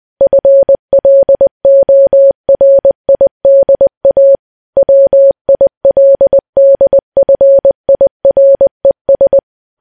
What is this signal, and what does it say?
Morse code, heard as a keyed tone, about 20 wpm, 550 Hz. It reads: FLORIDA WILDFIRES
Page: news-headlines-in-morse-code-at-20-wpm